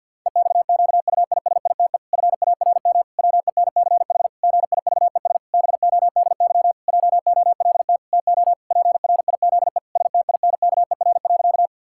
Morse code, heard as a keyed tone, about 53 wpm, 700 Hz. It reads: EXQUISITE FARM WENCH GIVES BODY JOLT TO PRIZE STINKER.
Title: Pangram Challenge v2 - 50wpm